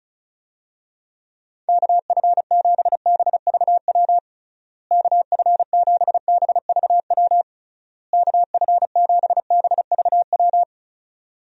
Morse code, heard as a keyed tone, about 36 wpm, 700 Hz. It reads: KF7BVW KF7BVW KF7BVW
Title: USA Call Signs - 6 Characters - Repeated 3x - 35wpm